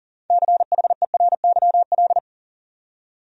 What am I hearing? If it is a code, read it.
Morse code, heard as a keyed tone, about 40 wpm, 700 Hz. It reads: CHERYL
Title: USA Common Names - 40wpm